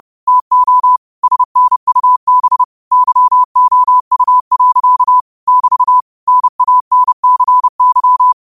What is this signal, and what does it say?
Morse code, heard as a keyed tone, about 30 wpm, 1000 Hz. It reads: TO SNUB YOU. <BT> NANCY
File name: qotdfiles / 30wpm